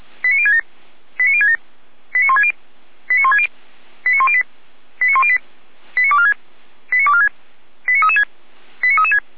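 Five-tone selective call (ZVEI), heard as quick runs of five tones, tones of about 70 ms. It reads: 89067 89067 89160 89160 89108 89108 89246 89246 89336 89336
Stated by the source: ZVEI five-tone call 89560 received by SDR.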